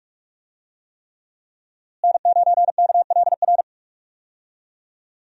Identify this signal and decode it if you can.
Morse code, heard as a keyed tone, about 41 wpm, 700 Hz. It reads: N9KLR
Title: USA Call Signs - 5 Characters - Repeated 3x - 45wpm